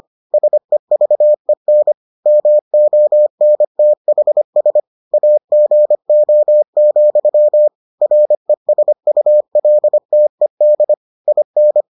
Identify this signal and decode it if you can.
Morse code, heard as a keyed tone, about 25 wpm, 600 Hz. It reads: SEVEN MONTHS AGO, RESULTED IN